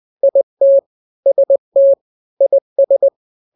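Morse code, heard as a keyed tone, about 20 wpm, 550 Hz. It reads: IT ST IS
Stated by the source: Morse drill of two-letter n-grams at 20/18 WPM (Farnsworth), from KK7RR.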